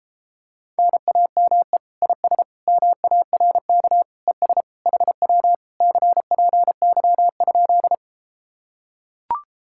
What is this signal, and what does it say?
Morse code, heard as a keyed tone, about 33 wpm, 700 Hz. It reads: NAME IS MARK ES HW CPY?